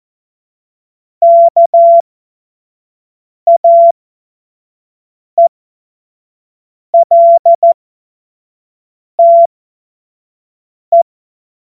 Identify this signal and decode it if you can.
Morse code, heard as a keyed tone, about 14 wpm, spaced as for 5 wpm, 680 Hz. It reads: KAELTE
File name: Morsekurs / Witze / Witz 08. August 2023 25 F70